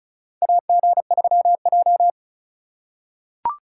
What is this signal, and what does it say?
Morse code, heard as a keyed tone, about 34 wpm, 700 Hz. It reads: AG3J
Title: USA Call Signs - 4 Characters - 8x Character Spacing - 35wpm